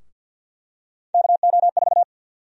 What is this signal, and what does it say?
Morse code, heard as a keyed tone, about 52 wpm, 700 Hz. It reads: KO4
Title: USA Call Sign Prefixes - 50wpm